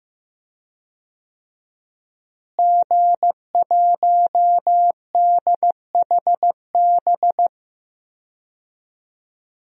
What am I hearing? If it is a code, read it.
Morse code, heard as a keyed tone, about 15 wpm, 700 Hz. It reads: G1DHB
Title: DXCC-100 Call Signs - 5 Characters - Repeated 3x - 15wpm